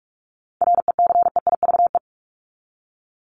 Morse code, heard as a keyed tone, about 44 wpm, 700 Hz. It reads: RECEIVE
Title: Top 751-1000 Words 45wpm